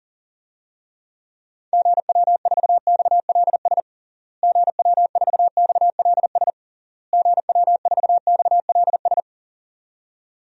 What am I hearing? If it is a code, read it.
Morse code, heard as a keyed tone, about 40 wpm, 700 Hz. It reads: GW4XLS GW4XLS GW4XLS
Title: DXCC-100 Call Signs - 6 Characters - Repeated 3x - 40wpm